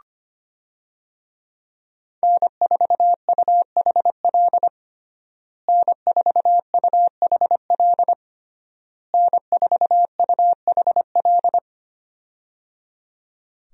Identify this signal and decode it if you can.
Morse code, heard as a keyed tone, about 25 wpm, 700 Hz. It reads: N4UHL N4UHL N4UHL
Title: USA Call Signs - 5 Characters - Repeated 3x - 25wpm